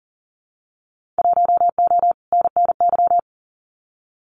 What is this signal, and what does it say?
Morse code, heard as a keyed tone, about 40 wpm, 700 Hz. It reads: 1O NNY
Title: Winter Field Day Exchanges 40wpm